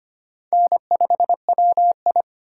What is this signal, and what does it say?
Morse code, heard as a keyed tone, about 31 wpm, 700 Hz. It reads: N5WI